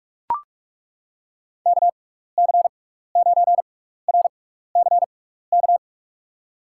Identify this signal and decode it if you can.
Morse code, heard as a keyed tone, about 41 wpm, spaced as for 15 wpm, 700 Hz. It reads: KC9RCK
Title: USA Call Signs - 6 Characters - 6x Character Spacing - 45wpm